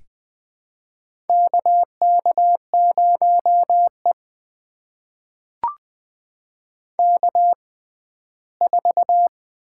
Morse code, heard as a keyed tone, about 20 wpm, spaced as for 14 wpm, 700 Hz. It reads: KK0E K4
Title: USA Call Signs - 4 Characters - 6x Character Spacing - 20wpm